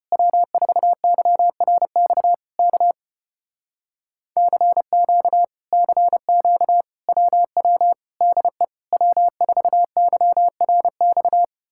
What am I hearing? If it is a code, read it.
Morse code, heard as a keyed tone, about 33 wpm, 700 Hz. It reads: W4YRX K CQ CQ WW DE W4YRX